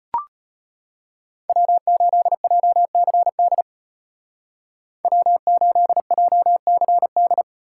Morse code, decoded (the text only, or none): W8JCD W8JCD